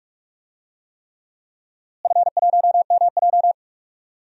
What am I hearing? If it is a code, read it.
Morse code, heard as a keyed tone, about 46 wpm, 700 Hz. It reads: F1MJ
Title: DXCC-100 Call Signs - 4 Characters - Repeated 3x - 45wpm